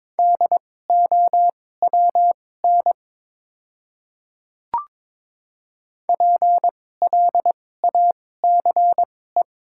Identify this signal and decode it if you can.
Morse code, heard as a keyed tone, about 22 wpm, spaced as for 16 wpm, 700 Hz. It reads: DOWN PLACE
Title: 2x Character Spacing - Top 200 Words - 22wpm